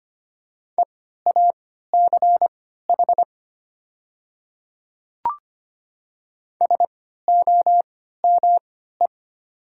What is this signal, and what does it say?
Morse code, heard as a keyed tone, about 25 wpm, spaced as for 14 wpm, 700 Hz. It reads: EACH SOME
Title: 3x Character Spacing - Top 100 Words - 25wpm